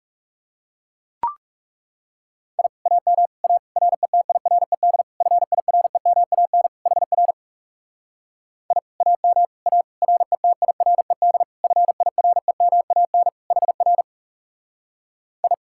Morse code, heard as a keyed tone, about 37 wpm, 700 Hz. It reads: I AM A RETIRED FIREMAN HR I AM A RETIRED FIREMAN HR I
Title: Speed-Racing Common QSO Phrases - 30wpm